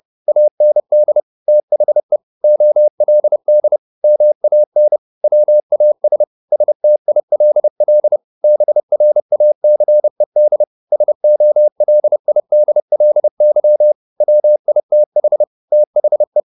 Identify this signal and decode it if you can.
Morse code, heard as a keyed tone, about 30 wpm, 600 Hz. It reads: AND THE OLD MAN WAS STILL BRACED SOLIDLY WITH THE
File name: the old man and the sea / the old man and the sea 1 30wpm